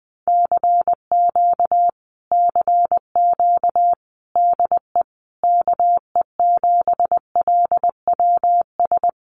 Morse code, heard as a keyed tone, about 20 wpm, 700 Hz. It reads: CQ CQ DE KE7LWS
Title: Common QSO Phrases Encore 20wpm